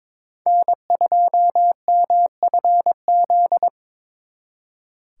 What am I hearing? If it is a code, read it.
Morse code, heard as a keyed tone, about 26 wpm, 700 Hz. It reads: N2MFZ
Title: Speed Racing - USA 1x3 Call Signs 22wpm